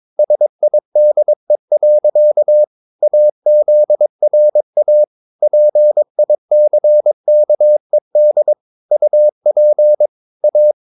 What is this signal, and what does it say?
Morse code, heard as a keyed tone, about 22 wpm, 600 Hz. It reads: SIDE. AZRA PICKED UP A
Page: Romanen Never af Ken Follet på engelsk.